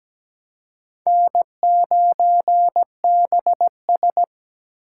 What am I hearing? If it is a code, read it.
Morse code, heard as a keyed tone, about 20 wpm, 700 Hz. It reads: N9BS
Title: Speed Racing - USA 1x2 Call Signs 17wpm